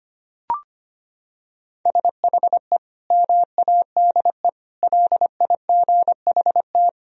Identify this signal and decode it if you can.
Morse code, heard as a keyed tone, about 25 wpm, 700 Hz. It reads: SHE MADE LIGHT
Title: Sentences from Top 1000 Words - Review Part 1 - 25wpm